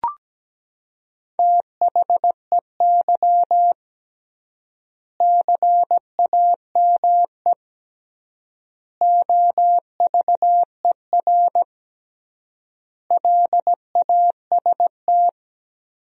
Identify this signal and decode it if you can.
Morse code, heard as a keyed tone, about 17 wpm, 700 Hz. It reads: THEY CAME OVER LAST
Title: 3x Word-Spacing - Sentences from Top 300 Words - 17wpm